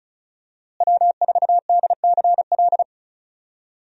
Morse code, heard as a keyed tone, about 34 wpm, 700 Hz. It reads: W4DCL
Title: USA Call Signs - 5 Characters - Repeated 2x - 35wpm